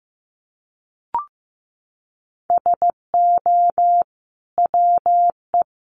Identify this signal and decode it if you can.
Morse code, heard as a keyed tone, about 15 wpm, 700 Hz. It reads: SO WE
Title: Sentences from Top 1500 Words 15wpm